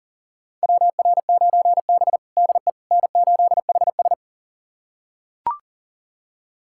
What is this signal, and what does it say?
Morse code, heard as a keyed tone, about 37 wpm, 700 Hz. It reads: WR9B DE N8HS
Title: USA Callsign DE Callsign - 4 Characters - 40wpm